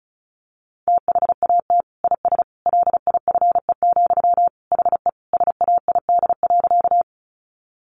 Morse code, heard as a keyed tone, about 34 wpm, 700 Hz. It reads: THAT IS LIFE, HE SAID.
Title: Sentences from Top 900 Words 35wpm